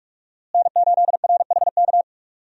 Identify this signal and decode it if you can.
Morse code, heard as a keyed tone, about 44 wpm, 700 Hz. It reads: N8RHK
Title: USA Call Signs - 5 Characters - Repeated 3x - 45wpm